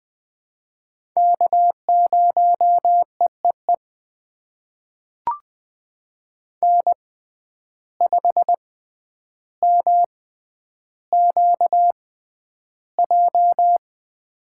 Morse code, heard as a keyed tone, about 20 wpm, spaced as for 10 wpm, 700 Hz. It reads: K0EEE N5MQJ
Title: USA Call Signs - 5 Characters - 6x Character Spacing - 20wpm